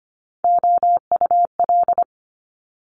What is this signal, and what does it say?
Morse code, heard as a keyed tone, about 24 wpm, 700 Hz. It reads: OUL
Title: Multi-letter Phonemes Quick Fire 25wpm